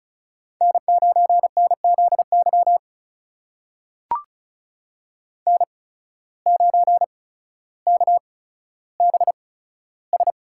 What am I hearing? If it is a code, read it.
Morse code, heard as a keyed tone, about 36 wpm, spaced as for 15 wpm, 700 Hz. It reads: N9NZY N9KBS
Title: USA Call Signs - 5 Characters - 8x Character Spacing - 35wpm